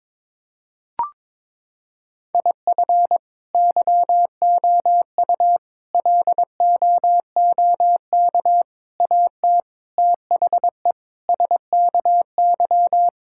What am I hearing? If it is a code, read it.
Morse code, heard as a keyed tone, about 22 wpm, 700 Hz. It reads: IF YOU LOOK AT THE SKY
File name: practice / 22wpm / Sentences with 4-Word Phrases - 22wpm